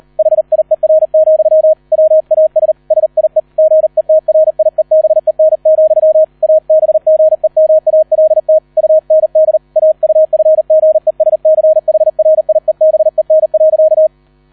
Morse code, heard as a keyed tone, about 38 wpm, 630 Hz, with radio noise. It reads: HIER, WAS SIE GETRIEBEN, ABGEMALT UND AUFGESCHRIEBEN.